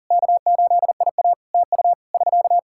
Morse code, heard as a keyed tone, about 37 wpm, 700 Hz. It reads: K8IA TU <SK>
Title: K1USN SST QSO Practice 40wpm